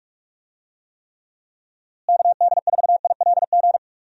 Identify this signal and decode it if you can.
Morse code, heard as a keyed tone, about 46 wpm, 700 Hz. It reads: KD4ILG
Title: USA Call Signs - 6 Characters - Repeated 2x - 45wpm